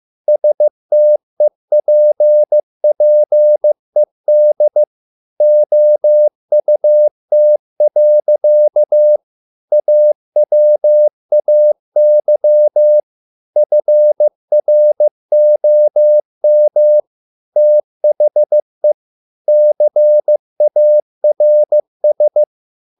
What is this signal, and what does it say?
Morse code, heard as a keyed tone, about 15 wpm, 600 Hz. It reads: STEPPED OUT. AWAY FROM THE CARS